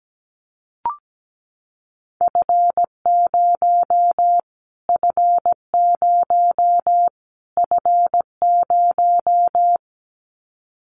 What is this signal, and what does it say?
Morse code, heard as a keyed tone, about 17 wpm, 700 Hz. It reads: F0 F0 F0
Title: Callsign Prefixes for Top 25 DXCC - Repeated 3x - 17wpm